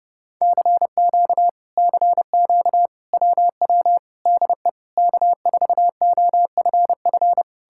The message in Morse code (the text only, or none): CQ CQ WW DE K4OFF